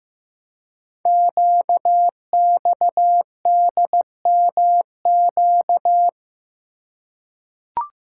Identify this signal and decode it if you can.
Morse code, heard as a keyed tone, about 15 wpm, 700 Hz. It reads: QXDMQ
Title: Code Groups - 5 Letters - 15wpm